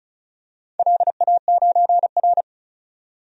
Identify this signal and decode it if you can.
Morse code, heard as a keyed tone, about 34 wpm, 700 Hz. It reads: LA9R
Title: DXCC-100 Call Signs - 4 Characters - 35wpm